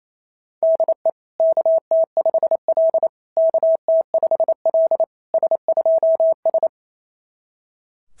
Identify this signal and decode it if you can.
Morse code, heard as a keyed tone, about 25 wpm, 650 Hz. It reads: DE KT5L KT5L S2S
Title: SOTA QSO Practice 28wpm